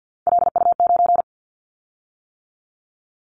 Morse code, heard as a keyed tone, about 50 wpm, 700 Hz. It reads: LU9
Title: Callsign Prefixes for Top 50 DXCC - 50wpm